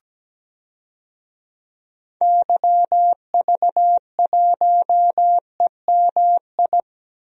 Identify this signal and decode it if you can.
Morse code, heard as a keyed tone, about 17 wpm, 700 Hz. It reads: YV1EMI
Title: DXCC-100 Call Signs - 6 Characters - 17wpm